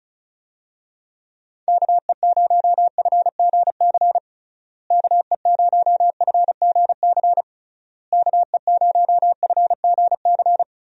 Morse code, heard as a keyed tone, about 34 wpm, 700 Hz. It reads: KE0FGC KE0FGC KE0FGC
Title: USA Call Signs - 6 Characters - Repeated 3x - 35wpm